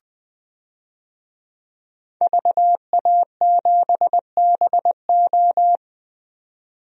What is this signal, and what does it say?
Morse code, heard as a keyed tone, about 20 wpm, 700 Hz. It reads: VA7BO
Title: DXCC-100 Call Signs - 5 Characters - 20wpm